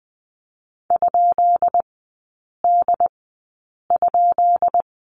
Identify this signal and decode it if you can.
Morse code, heard as a keyed tone, about 20 wpm, spaced as for 8 wpm, 690 Hz. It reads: ?D?
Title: 133 Lesson 22e - Question Mark - Three Characters 20wpm